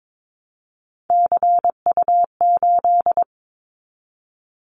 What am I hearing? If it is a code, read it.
Morse code, heard as a keyed tone, about 22 wpm, 700 Hz. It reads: CU8
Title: Callsign Prefixes for Top 50 DXCC - 22wpm